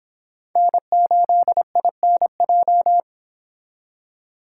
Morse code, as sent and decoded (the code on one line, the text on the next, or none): -. ---.. .. -. .---
N8INJ